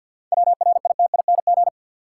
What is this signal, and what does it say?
Morse code, heard as a keyed tone, about 52 wpm, 700 Hz. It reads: WRITING
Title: Sentences from Top 800 Words - New Word Drill - 50wpm